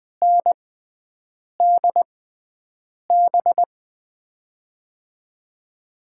Morse code, heard as a keyed tone, about 21 wpm, spaced as for 7 wpm, 700 Hz. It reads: NDB
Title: USA Call Signs - 5 Characters - 6x Character Spacing - 20wpm